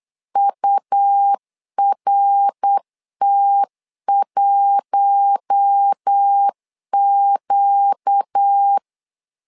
Morse code, heard as a keyed tone, about 8 wpm, 800 Hz. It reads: URT1Q
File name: LEZIONI / 4 Q1 8wpm 3035